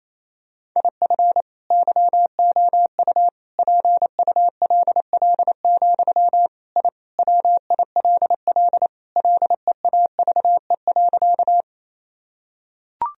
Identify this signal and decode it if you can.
Morse code, heard as a keyed tone, about 28 wpm, 700 Hz. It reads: IF YOU PULL, I WILL LEAVE.
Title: Sentences from Top 1000 Words - Review Part 2 - 28wpm